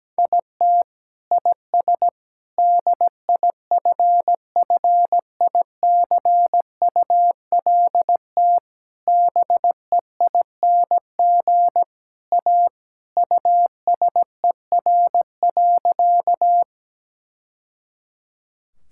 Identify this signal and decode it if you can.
Morse code, heard as a keyed tone, about 17 wpm, 700 Hz. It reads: IT IS DIFFICULT BEING A USER.
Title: Sentences from Top 1000 Words - Review Part 5 - 17wpm